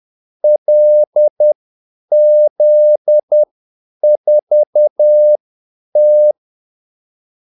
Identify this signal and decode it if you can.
Morse code, heard as a keyed tone, about 10 wpm, spaced as for 8 wpm, 600 Hz. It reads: LZ4T
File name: Callsigns-10-8wpm